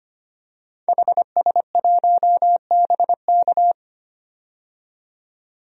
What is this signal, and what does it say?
Morse code, heard as a keyed tone, about 25 wpm, 700 Hz. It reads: HS1BK
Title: DXCC-100 Call Signs - 5 Characters - 25wpm